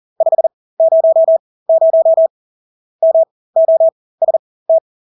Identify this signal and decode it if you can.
Morse code, heard as a keyed tone, about 40 wpm, spaced as for 20 wpm, 650 Hz. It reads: 500 MOST
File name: Engelske ord 40-20 wpm